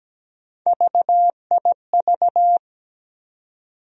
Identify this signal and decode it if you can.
Morse code, heard as a keyed tone, about 17 wpm, 700 Hz. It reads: VIV
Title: USA Call Sign Postfixes - 3 Letters 17wpm